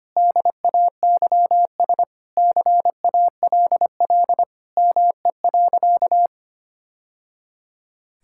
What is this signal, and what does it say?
Morse code, heard as a keyed tone, about 25 wpm, 700 Hz. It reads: DAYS CALL ME.
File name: practice / 25wpm / Sentences with 3-Word Phrases - 25wpm